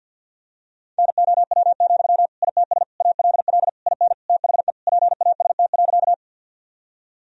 Morse code, heard as a keyed tone, about 52 wpm, 700 Hz. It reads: NOW, ITS ALL IN THE PAST.
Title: Sentences from Top 1000 Words - Review Part 3 - 50wpm